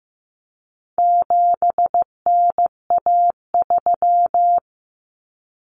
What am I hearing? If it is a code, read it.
Morse code, heard as a keyed tone, about 15 wpm, 700 Hz. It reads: 7NA3
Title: Common Cut Numbers - 4-Digit 15wpm